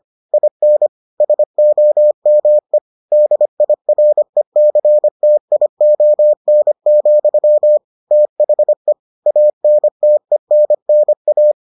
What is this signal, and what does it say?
Morse code, heard as a keyed tone, about 25 wpm, 600 Hz. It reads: IN SOME DIRECTION, THE ANTENNA